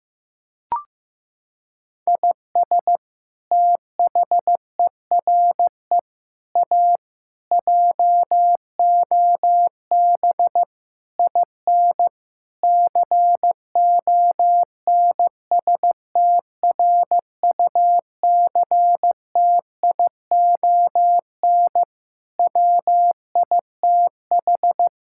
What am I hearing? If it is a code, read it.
Morse code, heard as a keyed tone, about 15 wpm, 700 Hz. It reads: IS THERE A JOB IN CONSTRUCTION WITH
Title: Sentences from Top 1600 Words 15wpm